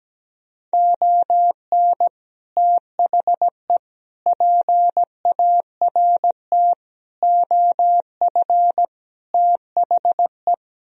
Morse code, heard as a keyed tone, about 17 wpm, 700 Hz. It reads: ON THE PART OF THE
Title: Speed-Racing - Sets of 5 Words 17wpm